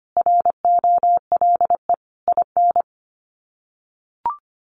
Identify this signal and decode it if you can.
Morse code, heard as a keyed tone, about 25 wpm, 700 Hz. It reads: ROLE IN
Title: Sets of 5 Words Encore2 25wpm